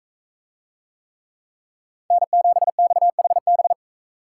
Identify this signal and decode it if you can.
Morse code, heard as a keyed tone, about 42 wpm, 700 Hz. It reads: N7XHB